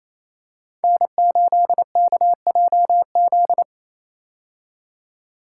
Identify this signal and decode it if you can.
Morse code, heard as a keyed tone, about 27 wpm, 700 Hz. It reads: N8KJZ
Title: Code Groups - 5 Letter:Numbers - 28wpm